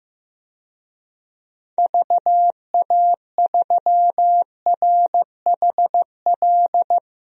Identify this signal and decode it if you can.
Morse code, heard as a keyed tone, about 15 wpm, 700 Hz. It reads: VA3RHL
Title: DXCC-100 Call Signs - 6 Characters - 15wpm